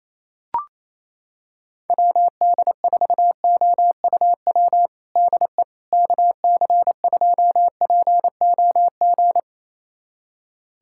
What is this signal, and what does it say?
Morse code, heard as a keyed tone, about 28 wpm, 700 Hz. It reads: WD4OUW DE KC2POG
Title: USA Callsign DE Callsign 28wpm